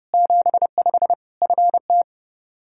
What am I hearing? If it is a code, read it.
Morse code, heard as a keyed tone, about 30 wpm, 700 Hz. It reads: 75 FT
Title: Practice QSO - Exchange 2 - Info - 30wpm